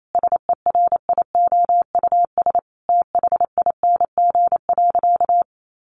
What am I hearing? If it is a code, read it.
Morse code, heard as a keyed tone, about 28 wpm, 700 Hz. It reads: SERIOUS THING.
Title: Sentences from Top 1500 Words 28wpm